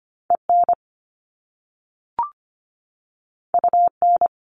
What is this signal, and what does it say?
Morse code, heard as a keyed tone, about 25 wpm, 700 Hz. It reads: EN UN
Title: Top Two Letter Combinations in Words 25wpm